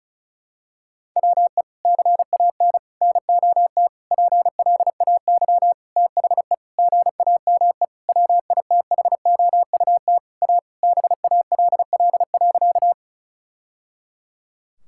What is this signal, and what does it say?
Morse code, heard as a keyed tone, about 36 wpm, 700 Hz. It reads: WE CAN NOT PLAY THE GAME WITHOUT A BALL.
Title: Sentences from Top 1000 Words - Review Part 6 - 35wpm